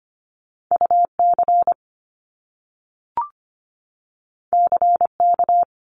Morse code, heard as a keyed tone, about 25 wpm, 700 Hz. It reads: UC CK
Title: Top Two Letter Combinations in Words 25wpm